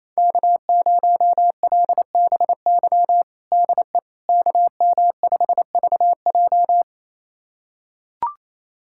Morse code, heard as a keyed tone, about 28 wpm, 700 Hz. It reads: K0LBY DE KM5VJ
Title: USA Callsign DE Callsign - 5 Characters - 28wpm